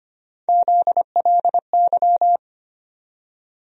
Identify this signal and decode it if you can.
Morse code, heard as a keyed tone, about 25 wpm, 700 Hz. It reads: ZLY
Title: USA Call Sign Postfixes - 3 Letters 25wpm